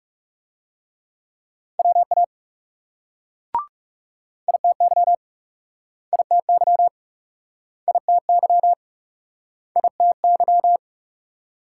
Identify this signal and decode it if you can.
Morse code, heard as a keyed tone, about 38 wpm, 700 Hz. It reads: WA ITY ITY ITY ITY